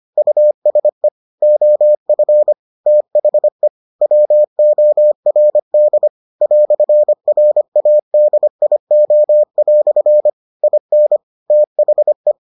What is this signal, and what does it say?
Morse code, heard as a keyed tone, about 25 wpm, 600 Hz. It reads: USE OF THE WORD "RADIO" IN THE